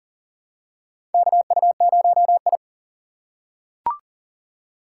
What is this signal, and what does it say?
Morse code, heard as a keyed tone, about 40 wpm, 700 Hz. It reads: KU0I
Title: USA Call Signs - 4 Characters - 3x Character Spacing - 40wpm